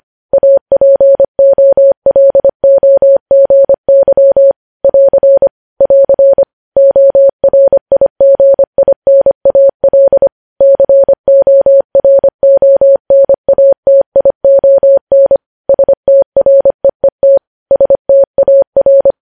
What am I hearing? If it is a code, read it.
Morse code, heard as a keyed tone, about 25 wpm, 550 Hz. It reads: APOLOGY <AR> <AR> ORIGINAL CORONATION STREET STAR
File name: podcast / News headlines-1x-25x25